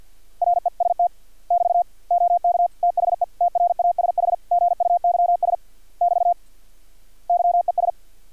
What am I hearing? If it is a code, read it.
Morse code, heard as a keyed tone, about 50 wpm, 700 Hz. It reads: REST <BT> OK THE TRASH GUYS <BT> YES
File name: archived / CRSnet